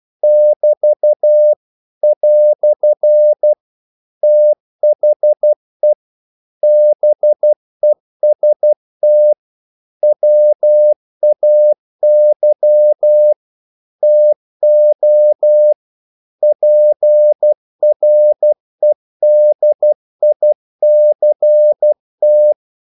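Morse code, heard as a keyed tone, about 12 wpm, 600 Hz. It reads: <BT> "THE BEST WAY TO PREDICT
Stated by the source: Morse practice sending English quotations.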